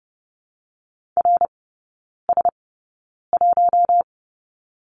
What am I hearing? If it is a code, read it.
Morse code, heard as a keyed tone, about 30 wpm, spaced as for 9 wpm, 700 Hz. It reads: RS1
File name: Lesson 18e - Letter m - Three Characters 30wpm